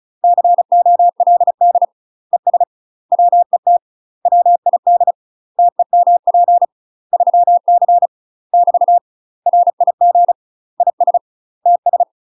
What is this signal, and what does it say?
Morse code, heard as a keyed tone, about 34 wpm, 700 Hz. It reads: COLD ES WET WID TEMP 3C <BT> RIG IS TS
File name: UK MASTER QSOs TRAINING / 35WPM / UK MASTER QSOs 0024